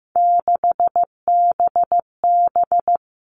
Morse code, heard as a keyed tone, about 15 wpm, 700 Hz. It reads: 6BB
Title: All Cut Numbers - 3-Digit 15wpm